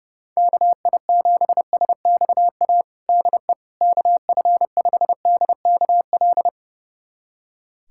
Morse code, heard as a keyed tone, about 30 wpm, 700 Hz. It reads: KI7SXA DE KF5DKL
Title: USA Callsign DE Callsign - 6 Characters - 30wpm